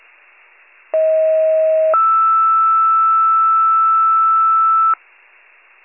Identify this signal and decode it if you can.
Two-tone page, heard as a one-second tone, then a three-second tone, about 640 then 1320 Hz. It is heard from 0.9 to 4.9 s.
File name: wcalert